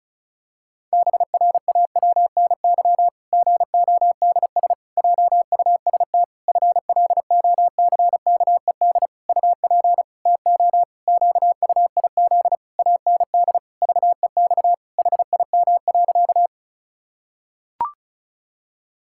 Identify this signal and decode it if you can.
Morse code, heard as a keyed tone, about 36 wpm, 700 Hz. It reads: BRAWNY GODS JUST FLOCKED UP TO QUIZ AND VEX HIM.
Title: Pangram Challenge v2 - 35wpm